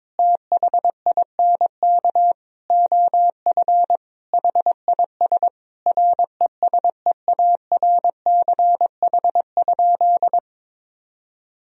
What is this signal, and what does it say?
Morse code, heard as a keyed tone, about 22 wpm, 700 Hz. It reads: THINK OF HIS RESEARCH?
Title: Sentences from Top 1000 Words - Review Part 4 - 22wpm